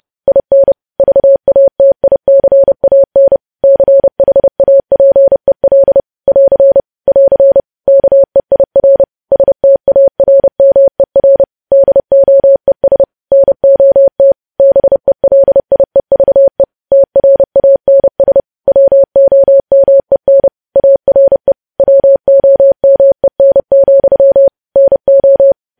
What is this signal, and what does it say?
Morse code, heard as a keyed tone, about 30 wpm, 550 Hz. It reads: IN VATICAN CHAPEL <AR> <AR> KEIR STARMER DOES NOT BELIEVE TRANS WOMEN ARE WOMEN, NO
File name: podcast / News headlines-1x-30x30